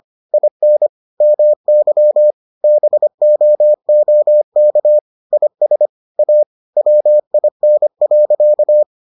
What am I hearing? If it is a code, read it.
Morse code, heard as a keyed tone, about 25 wpm, 600 Hz. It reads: IN MY BOOK IS A WIN.